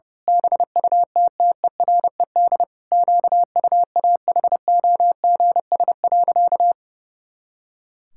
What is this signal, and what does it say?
Morse code, heard as a keyed tone, about 30 wpm, 700 Hz. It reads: BUTTERED QUAHOGS.
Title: Pangram Challenge v2 - 30wpm